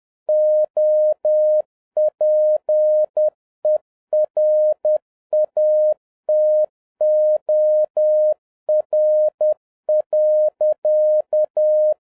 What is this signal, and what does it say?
Morse code, heard as a keyed tone, about 10 wpm, 610 Hz. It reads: OPERATOR.